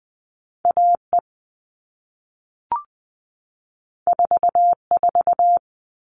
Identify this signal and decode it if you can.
Morse code, heard as a keyed tone, about 20 wpm, 700 Hz. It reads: AE 44
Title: All Cut Numbers - 2-Digit 20wpm